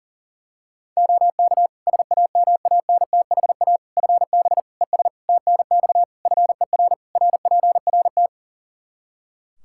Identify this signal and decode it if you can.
Morse code, heard as a keyed tone, about 37 wpm, 700 Hz. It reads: OK SAMANTHA FB ES TNX FER RPRT
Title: Practice QSO - Exchange 2 - Info - 40wpm